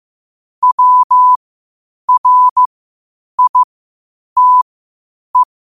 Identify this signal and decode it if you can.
Morse code, heard as a keyed tone, about 15 wpm, spaced as for 8 wpm, 1000 Hz. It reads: WRITE